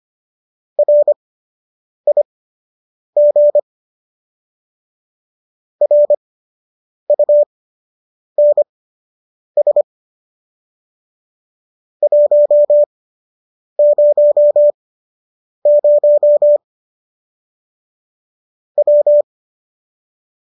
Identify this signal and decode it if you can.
Morse code, heard as a keyed tone, about 25 wpm, spaced as for 8 wpm, 590 Hz. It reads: RIG RUNS 100 W
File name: qso006 08